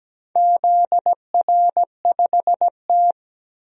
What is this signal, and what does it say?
Morse code, heard as a keyed tone, about 17 wpm, 700 Hz. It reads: ZR5T
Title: DXCC-100 Call Signs - 4 Characters - 17wpm